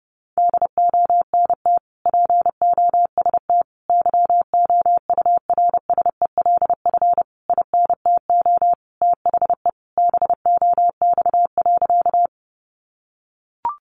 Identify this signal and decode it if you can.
Morse code, heard as a keyed tone, about 30 wpm, 700 Hz. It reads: DONT POST YOURSELF INTO THE BOX.
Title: Sentences from Top 1000 Words 30wpm